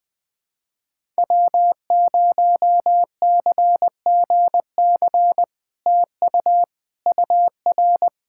Morse code, heard as a keyed tone, about 20 wpm, 700 Hz. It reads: W0CGC TU UR
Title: POTA QSO Practice - 20wpm